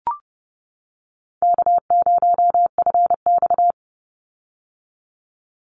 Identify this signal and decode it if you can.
Morse code, heard as a keyed tone, about 30 wpm, 700 Hz. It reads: K0FX
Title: Call Signs - 30wpm